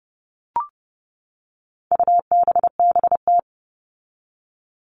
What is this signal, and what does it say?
Morse code, heard as a keyed tone, about 30 wpm, 700 Hz. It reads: UBBT
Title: All Cut Numbers - 4-Digit 30wpm